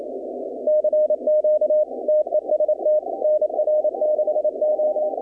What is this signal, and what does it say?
Morse code, heard as a keyed tone, about 28 wpm, 610 Hz. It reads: CQ TEST NR6O
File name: contests / 2019-cw-wpx-kt5j-k5pi- -2330 0-20